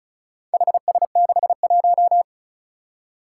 Morse code, heard as a keyed tone, about 36 wpm, 700 Hz. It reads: HS61
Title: Four Challenging Letter-Number Combinations 35wpm